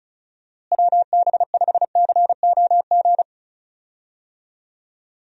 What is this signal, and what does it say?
Morse code, heard as a keyed tone, about 36 wpm, 700 Hz. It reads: WB5COG
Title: USA Call Signs - 6 Characters - Repeated 3x - 35wpm